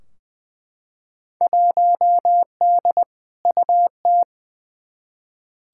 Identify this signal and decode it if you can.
Morse code, heard as a keyed tone, about 20 wpm, 700 Hz. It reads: 1D UT
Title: Field Day Exchanges - 20wpm